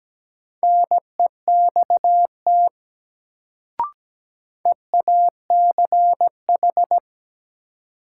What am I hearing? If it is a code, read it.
Morse code, heard as a keyed tone, about 17 wpm, 700 Hz. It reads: NEXT EACH
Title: Sets of 2 Words - New Word Drill - 17wpm